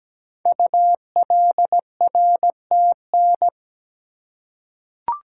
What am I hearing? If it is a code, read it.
Morse code, heard as a keyed tone, about 17 wpm, 700 Hz. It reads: ULRTN
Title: Code Groups - 5 Letters - 17wpm